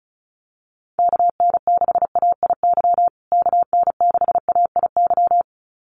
Morse code, heard as a keyed tone, about 34 wpm, 700 Hz. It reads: KN6AIY KN6AIY
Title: USA Call Signs Repeated 2x 35wpm